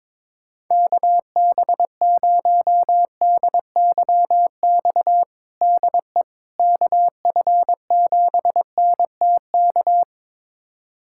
Morse code, heard as a keyed tone, about 22 wpm, 700 Hz. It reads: KB0DYX DE KF7NTK
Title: USA Callsign DE Callsign - 6 Characters - 22wpm